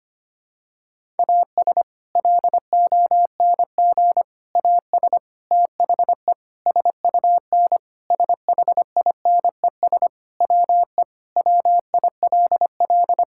Morse code, heard as a keyed tone, about 25 wpm, 700 Hz. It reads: AS LONG AS THE SUN SHINES WE WILL
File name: practice / 25wpm / Sentences with 4-Word Phrases - 25wpm